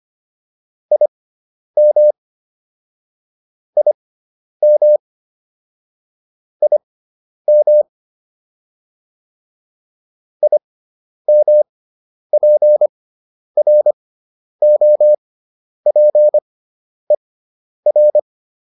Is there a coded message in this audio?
Yes: Morse code from 0.9 to 18.2 s, heard as a keyed tone, about 25 wpm, spaced as for 10 wpm, 600 Hz. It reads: IM IM IM IMPROPER